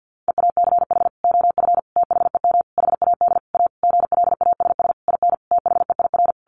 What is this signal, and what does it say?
Morse code, heard as a keyed tone, about 50 wpm, 700 Hz. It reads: EACH OF THEM HAD A GLASS IN THEIR
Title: Sentences from Top 1300 Words 50wpm